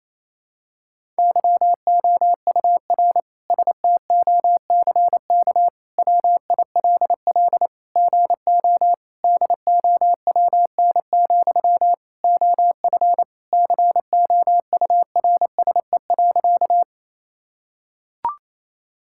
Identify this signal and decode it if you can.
Morse code, heard as a keyed tone, about 28 wpm, 700 Hz. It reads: YOUR STOCK WILL GO DOWN, OF COURSE.
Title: Sentences from Top 1000 Words - Review Part 5 - 28wpm